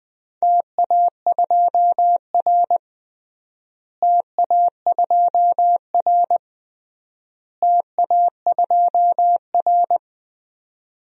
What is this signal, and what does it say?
Morse code, heard as a keyed tone, about 21 wpm, 700 Hz. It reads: TA2R TA2R TA2R
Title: DXCC-100 Call Signs - 4 Characters - Repeated 3x - 20wpm